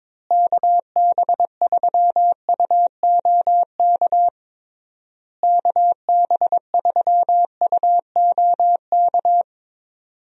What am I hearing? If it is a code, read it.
Morse code, heard as a keyed tone, about 22 wpm, 700 Hz. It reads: KB3UOK KB3UOK
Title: USA Call Signs - 6 Characters - Repeated 3x - 22wpm